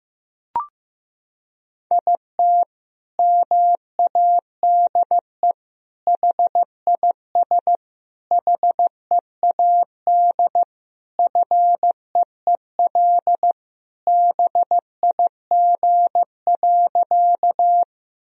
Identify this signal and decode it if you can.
Morse code, heard as a keyed tone, about 16 wpm, 700 Hz. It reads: IT MADE HIS HEAD FEEL BIG.
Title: Sentences from Top 1000 Words - Review Part 1 - 15wpm